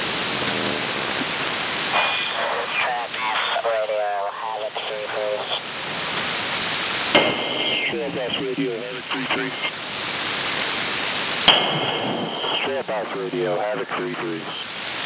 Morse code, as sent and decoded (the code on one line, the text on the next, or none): none